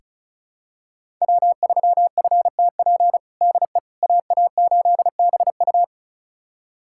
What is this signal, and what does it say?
Morse code, heard as a keyed tone, about 34 wpm, 700 Hz. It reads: W3FTP DE AA8BU
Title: USA Callsign DE Callsign - 5 Characters - 35wpm